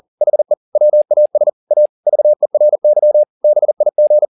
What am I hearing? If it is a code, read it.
Morse code, heard as a keyed tone, about 40 wpm, 600 Hz. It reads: HE WAS A VERY BIG
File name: the old man and the sea 2 40wpm